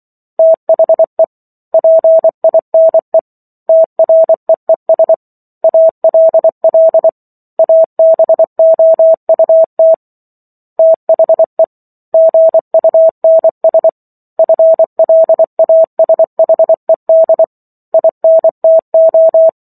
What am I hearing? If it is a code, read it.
Morse code, heard as a keyed tone, about 24 wpm, 650 Hz. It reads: THE PINE TREES ALL ABOUT THE GUNS FLASHED INTO
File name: Chapter 15 - 24 WPM